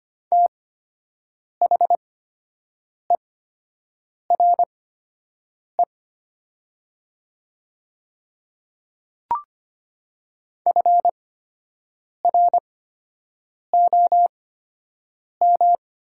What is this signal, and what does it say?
Morse code, heard as a keyed tone, about 25 wpm, spaced as for 7 wpm, 700 Hz. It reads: THERE FROM
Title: 8x Character Spacing - Top 100 Words - 25wpm